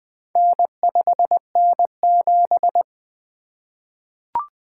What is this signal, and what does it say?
Morse code, heard as a keyed tone, about 20 wpm, 700 Hz. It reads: N5N7
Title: Four Letter-Number Combinations - 20wpm